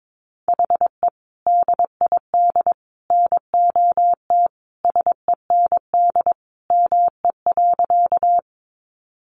Morse code, heard as a keyed tone, about 22 wpm, 700 Hz. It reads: HE DID NOT SEND ME.
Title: Sentences from Top 1500 Words 22wpm